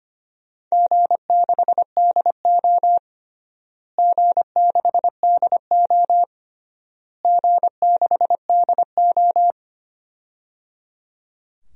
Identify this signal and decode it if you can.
Morse code, heard as a keyed tone, about 25 wpm, 700 Hz. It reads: G6DO G6DO G6DO
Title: DXCC-100 Call Signs - 4 Characters - Repeated 3x - 25wpm